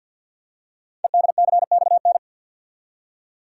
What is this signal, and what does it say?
Morse code, heard as a keyed tone, about 47 wpm, 700 Hz. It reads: EDCXN
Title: Code Groups - 5 Letters - 50wpm